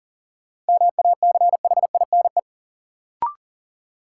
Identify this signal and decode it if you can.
Morse code, heard as a keyed tone, about 40 wpm, 700 Hz. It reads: MACHINE
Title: Sentences from Top 1000 Words - Review Part 2 - New Word Drill - 40wpm